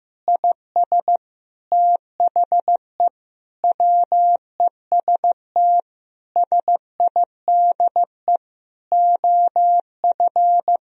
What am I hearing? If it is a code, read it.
Morse code, heard as a keyed tone, about 15 wpm, 700 Hz. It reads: IS THE WEST SIDE OF
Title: Sentences from Top 1000 Words - Review Part 2 - 15wpm